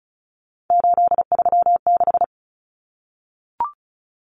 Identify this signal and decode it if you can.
Morse code, heard as a keyed tone, about 34 wpm, 700 Hz. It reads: 836
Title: Numbers - 3-Digit 35wpm